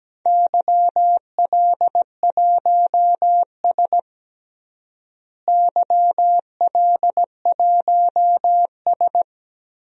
Morse code, heard as a keyed tone, about 17 wpm, 700 Hz. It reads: YL1S YL1S
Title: DXCC-100 Call Signs - 4 Characters - Repeated 2x - 17wpm